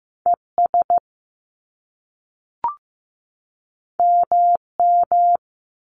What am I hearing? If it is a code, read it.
Morse code, heard as a keyed tone, about 15 wpm, 700 Hz. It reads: ES MM
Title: Top Two Letter Combinations in Words 15wpm